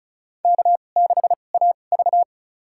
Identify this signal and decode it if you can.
Morse code, heard as a keyed tone, about 36 wpm, spaced as for 25 wpm, 690 Hz. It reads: K6AV
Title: USA Call Signs - 4 Characters - 2x Character Spacing - 35wpm